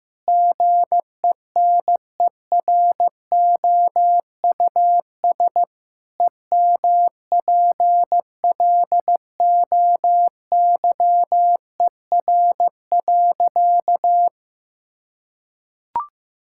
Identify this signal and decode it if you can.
Morse code, heard as a keyed tone, about 16 wpm, 700 Hz. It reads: GENEROUS EMPLOYER.
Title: Pangram Challenge v2 - 15wpm